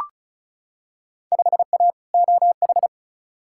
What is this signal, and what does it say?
Morse code, heard as a keyed tone, about 36 wpm, 700 Hz. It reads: 5A OH
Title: Field Day Exchanges Encore - 35wpm